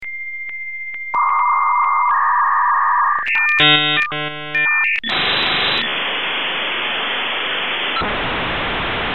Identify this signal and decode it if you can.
Dial-up modem handshake.
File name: virtual-virtual-reality-ost / rbdqgytpwp / 28 Dial Up Tone